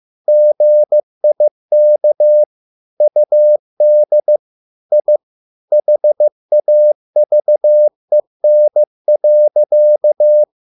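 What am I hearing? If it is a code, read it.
Morse code, heard as a keyed tone, about 15 wpm, 600 Hz. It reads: GIK UD I HAVEN.
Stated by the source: Morse code at a slow 15 wpm from CW med Gnister.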